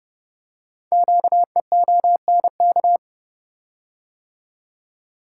Morse code, heard as a keyed tone, about 29 wpm, 700 Hz. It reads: QEONK
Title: Code Groups - 5 Letters - 30wpm